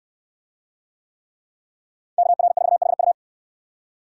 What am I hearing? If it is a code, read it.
Morse code, heard as a keyed tone, about 68 wpm, 700 Hz. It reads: KR4HU